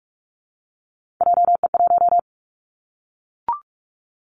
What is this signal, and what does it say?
Morse code, heard as a keyed tone, about 44 wpm, 700 Hz. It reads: JE1
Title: Callsign Prefixes for Top 25 DXCC - 45wpm